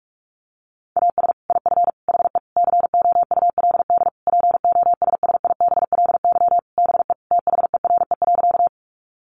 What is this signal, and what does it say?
Morse code, heard as a keyed tone, about 44 wpm, 700 Hz. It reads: AS IF HE COULD POSSIBLY BE THERE.
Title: Sentences from Top 1300 Words 45wpm